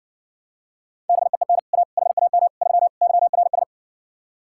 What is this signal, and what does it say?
Morse code, heard as a keyed tone, about 69 wpm, 700 Hz. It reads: BEEN A HAM 3 YRS